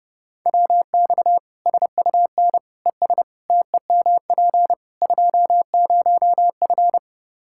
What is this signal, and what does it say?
Morse code, heard as a keyed tone, about 30 wpm, 700 Hz. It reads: WX SUN ES TEMP 20F
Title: Practice QSO - Exchange 2 - Info - 30wpm